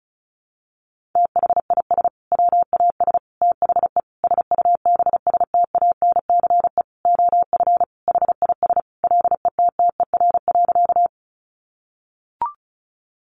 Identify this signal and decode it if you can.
Morse code, heard as a keyed tone, about 34 wpm, 700 Hz. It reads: THIS WAS THE SUBSTANCE OF HIS LETTER.
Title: Sentences from Top 1500 Words 35wpm